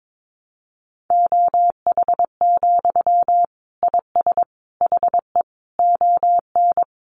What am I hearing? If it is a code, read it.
Morse code, heard as a keyed tone, about 22 wpm, 700 Hz. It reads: OH, IS HE ON
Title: Sentences from Top 500 Words 22wpm